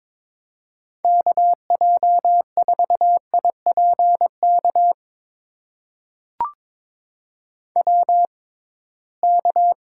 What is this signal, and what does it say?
Morse code, heard as a keyed tone, about 22 wpm, spaced as for 9 wpm, 700 Hz. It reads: KJ4IP KWK
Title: USA Call Signs - 6x Character Spacing - 22wpm